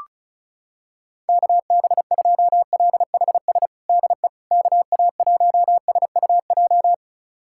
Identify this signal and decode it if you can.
Morse code, heard as a keyed tone, about 36 wpm, 700 Hz. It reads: KB2LHS DE KA1SUJ
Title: USA Callsign DE Callsign 35wpm